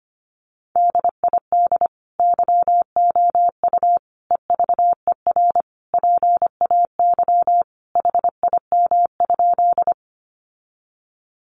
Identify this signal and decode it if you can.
Morse code, heard as a keyed tone, about 25 wpm, 700 Hz. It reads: DID YOU EVER PAY HIM?
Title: Sentences from Top 700 Words 25wpm